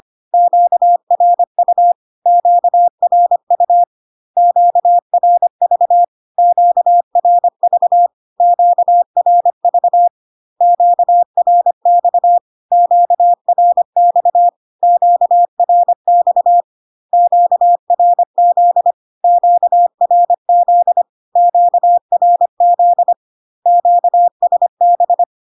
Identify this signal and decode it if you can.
Morse code, heard as a keyed tone, about 25 wpm, 700 Hz. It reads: QRU QRU QRV QRV QRV QRX QRX QRX QRZ QRZ QRZ QSB
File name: Q-codes most common repeated 3 times 25 25wpm